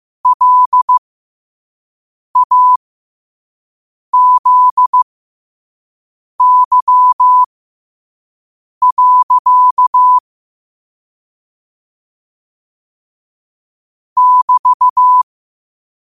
Morse code, heard as a keyed tone, about 15 wpm, spaced as for 5 wpm, 1000 Hz. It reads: LAZY. <BT>
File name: qotdfiles / 5wpm